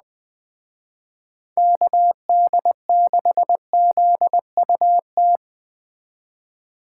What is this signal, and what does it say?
Morse code, heard as a keyed tone, about 20 wpm, 700 Hz. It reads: KD6ZUT
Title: Speed Racing - USA 2x3 Call Signs 17wpm